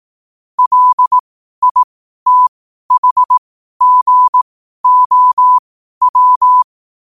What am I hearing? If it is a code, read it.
Morse code, heard as a keyed tone, about 18 wpm, spaced as for 12 wpm, 1000 Hz. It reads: LITHGOW
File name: qotdfiles / 12wpm